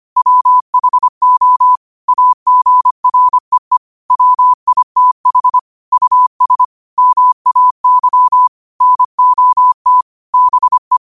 Morse code, heard as a keyed tone, about 25 wpm, 1000 Hz. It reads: WHO AGREE WITH US MAY NOT BE
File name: qotdfiles / 25wpm